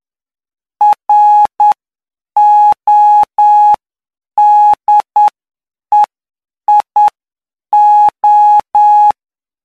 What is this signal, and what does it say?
Morse code, heard as a keyed tone, about 9 wpm, spaced as for 7 wpm, 840 Hz. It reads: RODEIO